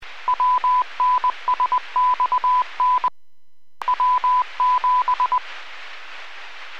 Morse code, heard as a keyed tone, about 20 wpm, 1000 Hz. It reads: WNSXN W7